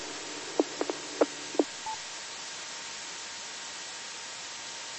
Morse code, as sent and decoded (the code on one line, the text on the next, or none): none